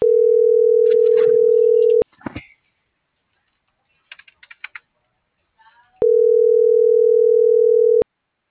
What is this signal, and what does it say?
Ringback tone (North America), 440 + 480 Hz.
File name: CENTRAL MULTIMODAL / out francisco.cobos - .12832